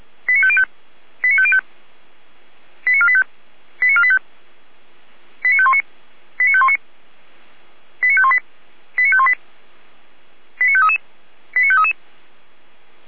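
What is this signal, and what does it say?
Five-tone selective call (ZVEI), heard as quick runs of five tones, tones of about 70 ms. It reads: 89505 89505 89485 89485 89519 89519 89518 89518 89533 89533
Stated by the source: ZVEI five-tone call received by SDR.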